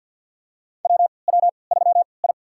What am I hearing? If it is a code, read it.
Morse code, heard as a keyed tone, about 49 wpm, spaced as for 28 wpm, 700 Hz. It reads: WW3I
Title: USA Call Signs - 3x Character Spacing - 50wpm